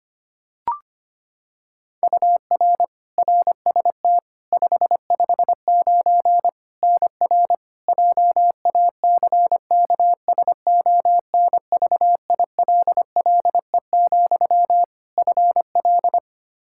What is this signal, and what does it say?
Morse code, heard as a keyed tone, about 25 wpm, 700 Hz. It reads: UR RST 559 NR JACKSONVILLE, FL
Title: Practice QSO - Exchange 1 - Beginning - 25wpm